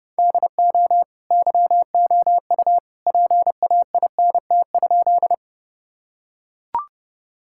Morse code, heard as a keyed tone, about 30 wpm, 700 Hz. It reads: DO YOU PAINT?
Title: Sentences from Top 1000 Words - Review Part 6 - 30wpm